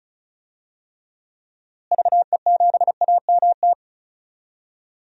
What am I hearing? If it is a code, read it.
Morse code, heard as a keyed tone, about 36 wpm, 700 Hz. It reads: VE7AMT
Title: DXCC-100 Call Signs - 6 Characters - Repeated 2x - 35wpm